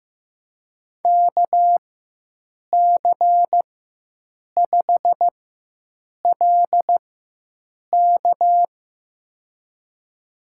Morse code, heard as a keyed tone, about 15 wpm, spaced as for 7 wpm, 700 Hz. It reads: KC5LK
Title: USA Call Signs - 5 Characters - 4x Character Spacing - 15wpm